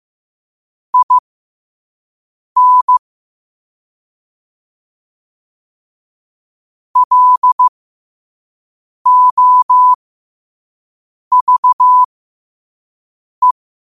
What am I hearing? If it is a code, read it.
Morse code, heard as a keyed tone, about 15 wpm, spaced as for 5 wpm, 1000 Hz. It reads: IN LOVE
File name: qotdfiles / 5wpm